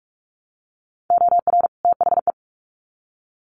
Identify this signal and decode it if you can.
Morse code, heard as a keyed tone, about 44 wpm, 700 Hz. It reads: OR THE
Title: Sets of 2 Words 45wpm